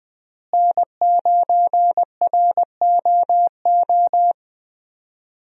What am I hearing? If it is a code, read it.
Morse code, heard as a keyed tone, about 20 wpm, 700 Hz. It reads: N9ROO
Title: USA Call Signs - 5 Characters - Repeated 3x - 20wpm